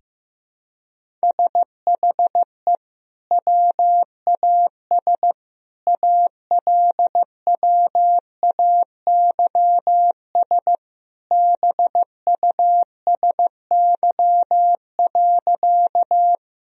Morse code, heard as a keyed tone, about 15 wpm, 700 Hz. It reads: SHE WAS ALWAYS BUSY.
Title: Sentences from Top 1000 Words - Review Part 3 - 15wpm